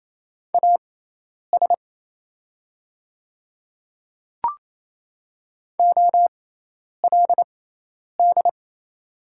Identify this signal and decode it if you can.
Morse code, heard as a keyed tone, about 28 wpm, spaced as for 10 wpm, 700 Hz. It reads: AS OLD
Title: 6x Character Spacing - Top 300 Words - 28wpm